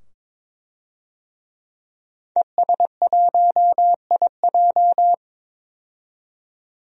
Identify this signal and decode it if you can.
Morse code, heard as a keyed tone, about 22 wpm, 700 Hz. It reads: ES1IJ
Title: DXCC-100 Call Signs - 5 Characters - 22wpm